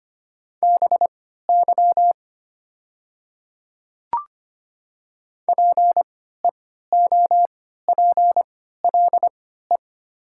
Morse code, heard as a keyed tone, about 25 wpm, spaced as for 14 wpm, 700 Hz. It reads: BY PEOPLE
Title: 3x Character Spacing - Top 200 Words - 25wpm